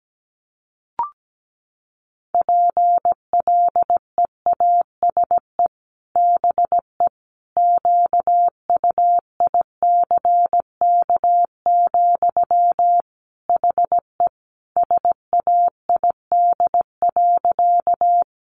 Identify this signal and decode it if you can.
Morse code, heard as a keyed tone, about 17 wpm, 700 Hz. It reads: PLEASE BE QUICK, HE SAID.
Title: Sentences from Top 1500 Words 17wpm